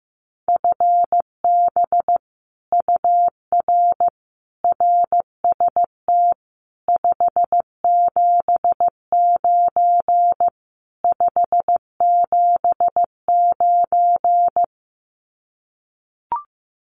Morse code, heard as a keyed tone, about 15 wpm, 700 Hz. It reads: FB UR RST 579 579
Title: Common QSO Phrases Encore 15wpm